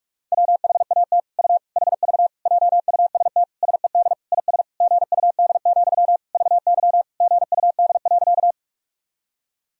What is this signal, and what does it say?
Morse code, heard as a keyed tone, about 47 wpm, 700 Hz. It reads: WHAT U HV JUST SED IS GUD, VY GUD.
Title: Sentences from Top 200 Words and Abbreviations 45wpm